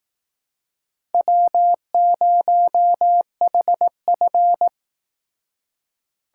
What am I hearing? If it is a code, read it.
Morse code, heard as a keyed tone, about 18 wpm, 700 Hz. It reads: W0HF
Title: Speed Racing - USA 1x2 Call Signs 15wpm